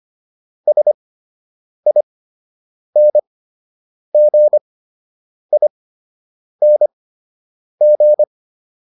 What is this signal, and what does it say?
Morse code, heard as a keyed tone, about 25 wpm, spaced as for 8 wpm, 600 Hz. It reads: SINGING